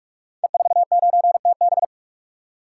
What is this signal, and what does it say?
Morse code, heard as a keyed tone, about 41 wpm, 700 Hz. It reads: E49TB
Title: Code Groups - 5 Letter:Numbers - 45wpm